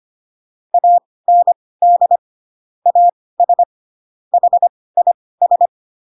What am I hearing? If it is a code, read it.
Morse code, heard as a keyed tone, about 25 wpm, spaced as for 18 wpm, 700 Hz. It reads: AND AS HIS